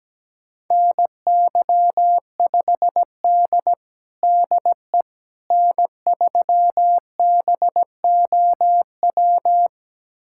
Morse code, heard as a keyed tone, about 17 wpm, 700 Hz. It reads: NY5D DE N3BOW
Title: USA Callsign DE Callsign 17wpm